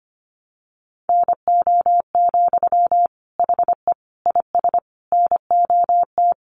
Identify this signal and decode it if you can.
Morse code, heard as a keyed tone, about 25 wpm, 700 Hz. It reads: NO, HE IS NOT
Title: Sentences from Top 1400 Words 25wpm